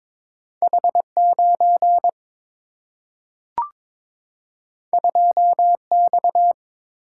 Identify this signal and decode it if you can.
Morse code, heard as a keyed tone, about 22 wpm, 700 Hz. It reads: H 9 2 X
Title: Single-Letter Single-Number 22wpm